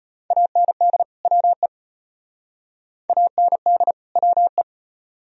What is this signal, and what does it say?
Morse code, heard as a keyed tone, about 32 wpm, 700 Hz. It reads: AND WE AND WE